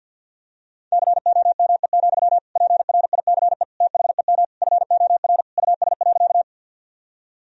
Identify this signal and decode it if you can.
Morse code, heard as a keyed tone, about 50 wpm, 700 Hz. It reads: COME, PRICE THEM FOR US.
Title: Sentences from Top 1000 Words - Review Part 4 - 50wpm